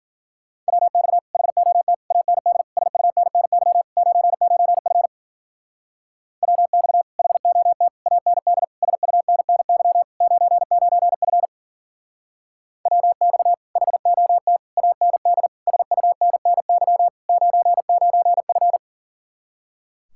Morse code, heard as a keyed tone, about 49 wpm, 700 Hz. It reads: WX HOT AND SUNNY 99F WX HOT AND SUNNY 99F WX HOT AND SUNNY 99F
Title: Speed-Racing Common QSO Phrases - 40wpm